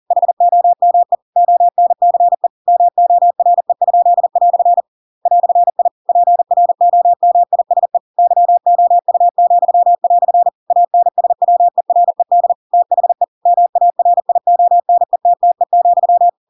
Morse code, heard as a keyed tone, about 40 wpm, 700 Hz. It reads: HOME ONCE MORE?" "I PROMISE YOU," ANSWERED THE MARIONETTE,